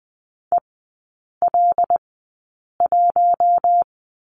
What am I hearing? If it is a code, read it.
Morse code, heard as a keyed tone, about 20 wpm, spaced as for 8 wpm, 700 Hz. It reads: EL1
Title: 074 Lesson 13e - Letter L - Three Characters 20wpm